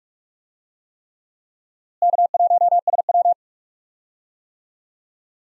Morse code, heard as a keyed tone, about 47 wpm, 700 Hz. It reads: K1SW
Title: USA Call Signs - 4 Characters - Repeated 3x - 45wpm